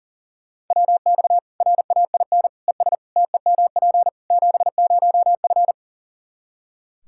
Morse code, heard as a keyed tone, about 37 wpm, 700 Hz. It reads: WX RAIN ES TEMP 70F
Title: Practice QSO - Exchange 2 - Info - 40wpm